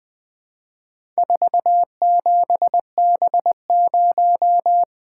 Morse code, heard as a keyed tone, about 20 wpm, 700 Hz. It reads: 47B0
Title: Four Letter-Number Combinations - 20wpm